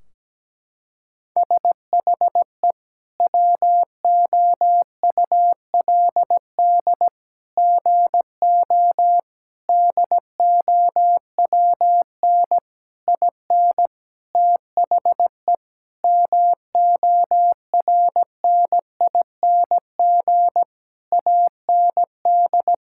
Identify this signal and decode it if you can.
Morse code, heard as a keyed tone, about 17 wpm, 700 Hz. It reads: SHE WOULD GO DOWN IN THE MORNING AND
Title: Sentences from Top 1600 Words 17wpm